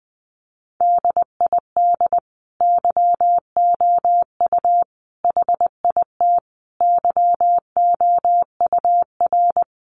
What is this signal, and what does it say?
Morse code, heard as a keyed tone, about 19 wpm, 700 Hz. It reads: DID YOU HIT YOUR
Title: Sentences from Top 1400 Words 20wpm